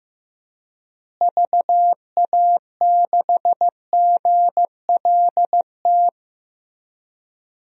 Morse code, heard as a keyed tone, about 15 wpm, 700 Hz. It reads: VA6GLT
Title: DXCC-100 Call Signs - 6 Characters - Repeated 2x - 15wpm